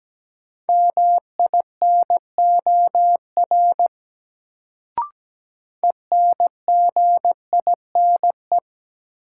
Morse code, heard as a keyed tone, about 17 wpm, 700 Hz. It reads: MINOR ENGINE
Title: Sentences from Top 1500 Words - New Word Drill Alt - 17wpm